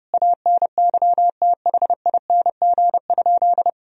Morse code, heard as a keyed tone, about 30 wpm, 700 Hz. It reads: ANYTHING?
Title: Sentences from Top 1000 Words - Review Part 4 - 30wpm